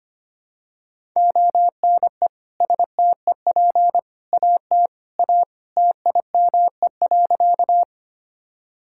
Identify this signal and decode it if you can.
Morse code, heard as a keyed tone, about 25 wpm, 700 Hz. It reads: ONE STEP AT A TIME.
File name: practice / 25wpm / Sentences with 3-Word Phrases - 25wpm